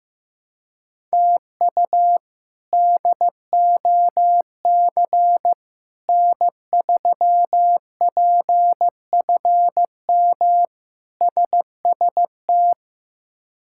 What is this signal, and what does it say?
Morse code, heard as a keyed tone, about 15 wpm, 700 Hz. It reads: TU DOC N3PFM SST
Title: K1USN SST QSO Practice - 15wpm